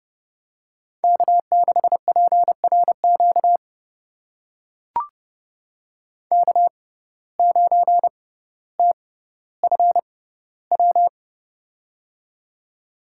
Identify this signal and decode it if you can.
Morse code, heard as a keyed tone, about 30 wpm, spaced as for 15 wpm, 700 Hz. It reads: K6PRQ K9TFW
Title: USA Call Signs - 5 Characters - 6x Character Spacing - 30wpm